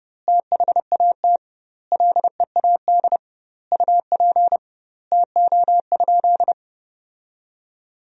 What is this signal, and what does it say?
Morse code, heard as a keyed tone, about 30 wpm, 700 Hz. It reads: THAT LEAD UP TO?
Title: 2x Word-Spacing - Sentences from Top 500 Words - 30wpm